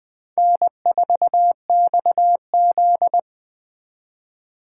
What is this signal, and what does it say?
Morse code, heard as a keyed tone, about 24 wpm, 700 Hz. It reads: N4XZ
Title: Speed Racing - USA Call Signs 20wpm